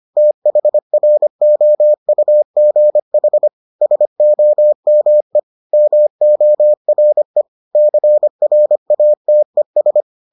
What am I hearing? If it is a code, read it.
Morse code, heard as a keyed tone, about 25 wpm, 600 Hz. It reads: THROUGH SOME MORE CRATES